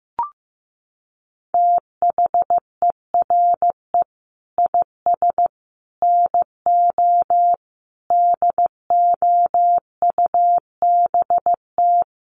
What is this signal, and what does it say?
Morse code, heard as a keyed tone, about 15 wpm, 700 Hz. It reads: THERE IS NO DOUBT
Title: Sets of 5 Words 15wpm